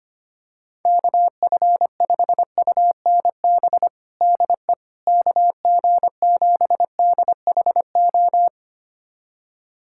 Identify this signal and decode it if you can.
Morse code, heard as a keyed tone, about 25 wpm, 700 Hz. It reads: KF5UNB DE KG7DHO
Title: USA Callsign DE Callsign - 6 Characters - 25wpm